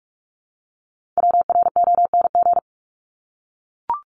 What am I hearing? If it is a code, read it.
Morse code, heard as a keyed tone, about 44 wpm, 700 Hz. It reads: WRONG
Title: Top 751-1000 Words 45wpm